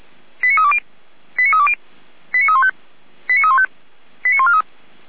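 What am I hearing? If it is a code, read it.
Five-tone selective call (ZVEI), heard as quick runs of five tones, tones of about 80 ms. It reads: 89320 89320 89315 89315 89143
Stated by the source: ZVEI five-tone selective call received by SDR.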